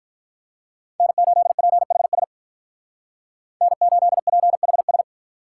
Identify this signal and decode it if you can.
Morse code, heard as a keyed tone, about 44 wpm, 700 Hz. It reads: N8PHS N8PHS